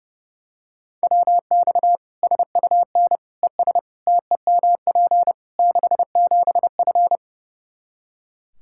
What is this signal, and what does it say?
Morse code, heard as a keyed tone, about 30 wpm, 700 Hz. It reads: WX SUN ES TEMP 67F
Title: Practice QSO - Exchange 2 - Info - 30wpm